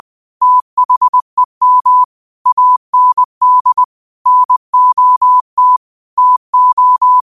Morse code, heard as a keyed tone, about 20 wpm, 1000 Hz. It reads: THEM AND NOT TO